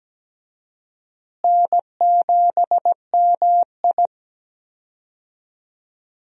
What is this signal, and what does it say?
Morse code, heard as a keyed tone, about 17 wpm, 700 Hz. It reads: N7MI
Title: USA Call Signs - 4 Characters - Repeated 2x - 17wpm